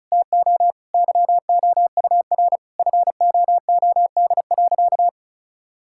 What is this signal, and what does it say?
Morse code, heard as a keyed tone, about 34 wpm, 700 Hz. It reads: TO YOUR FOOD.
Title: Sentences from Top 1000 Words - Review Part 5 - 35wpm